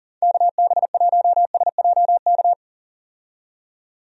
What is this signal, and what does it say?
Morse code, heard as a keyed tone, about 40 wpm, 700 Hz. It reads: KB1SJK
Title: USA Call Signs - 6 Characters - 40wpm